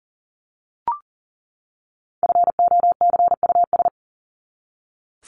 Morse code, heard as a keyed tone, about 38 wpm, 700 Hz. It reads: FOCUS
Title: Top 501-750 Words 40wpm